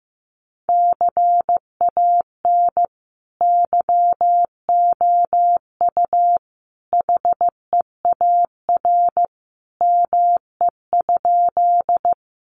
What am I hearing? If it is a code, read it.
Morse code, heard as a keyed tone, about 15 wpm, 700 Hz. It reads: CAN YOU HEAR ME?
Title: Sentences from Top 300 Words 15wpm